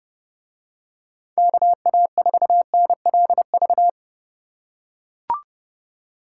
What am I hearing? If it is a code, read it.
Morse code, heard as a keyed tone, about 30 wpm, 700 Hz. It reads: KA4NLV